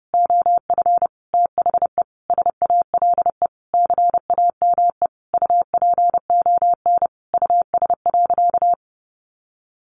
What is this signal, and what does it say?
Morse code, heard as a keyed tone, about 30 wpm, 700 Hz. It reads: OF THE SALE CAME UPON US.
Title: Sentences from Top 1400 Words 30wpm